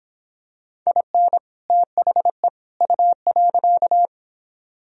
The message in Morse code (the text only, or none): IN THE U.